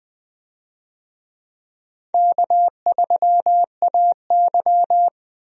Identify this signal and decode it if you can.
Morse code, heard as a keyed tone, about 20 wpm, 700 Hz. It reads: K3AY
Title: USA Call Signs - 4 Characters - Repeated 2x - 20wpm